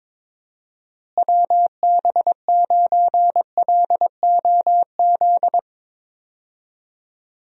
Morse code, heard as a keyed tone, about 22 wpm, 700 Hz. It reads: WB9LOZ
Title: USA Call Signs - 6 Characters - Repeated 2x - 22wpm